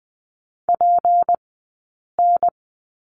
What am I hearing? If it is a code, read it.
Morse code, heard as a keyed tone, about 20 wpm, spaced as for 8 wpm, 700 Hz. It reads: PN
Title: 151 Lesson 25d - Letter p - Two Characters 20wpm